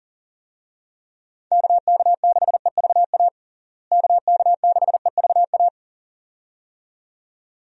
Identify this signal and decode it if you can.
Morse code, heard as a keyed tone, about 40 wpm, 700 Hz. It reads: KK6EVA KK6EVA
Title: USA Call Signs - 6 Characters - Repeated 2x - 40wpm